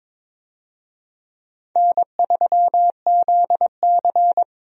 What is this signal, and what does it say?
Morse code, heard as a keyed tone, about 21 wpm, 700 Hz. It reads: N3ZC
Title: USA Call Signs - 4 Characters - 22wpm